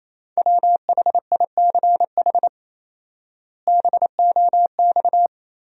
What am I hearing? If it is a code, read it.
Morse code, heard as a keyed tone, about 28 wpm, 700 Hz. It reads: WHICH BOX
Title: 4x Word-Spacing - Sentences from Top 500 Words - 28wpm